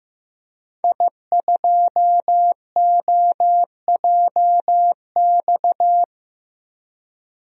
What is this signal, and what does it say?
Morse code, heard as a keyed tone, about 15 wpm, 700 Hz. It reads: I2OJX
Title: DXCC-100 Call Signs - 5 Characters - Repeated 3x - 15wpm